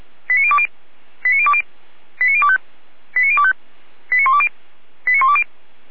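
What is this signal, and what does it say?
Five-tone selective call (ZVEI), heard as quick runs of five tones, tones of about 70 ms. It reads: 89020 89020 89025 89025 89120 89120
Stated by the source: Five-tone ZVEI selective call received by SDR.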